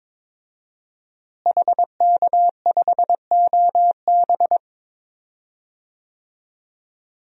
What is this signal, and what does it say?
Morse code, heard as a keyed tone, about 22 wpm, 700 Hz. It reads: HK5OB
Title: DXCC-100 Call Signs - 5 Characters - Repeated 2x - 22wpm